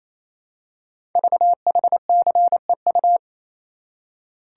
Morse code, heard as a keyed tone, about 27 wpm, 700 Hz. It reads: VHCEU
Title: Code Groups - 5 Letters - 28wpm